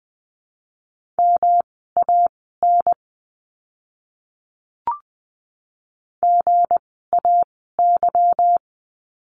Morse code, heard as a keyed tone, about 19 wpm, spaced as for 14 wpm, 700 Hz. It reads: MAN GAY
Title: 160 Lesson 26f - Letter g - Words 20wpm